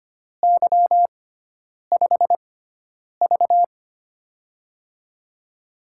Morse code, heard as a keyed tone, about 25 wpm, spaced as for 9 wpm, 700 Hz. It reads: Y5V
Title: USA Call Signs - 4 Characters - 6x Character Spacing - 25wpm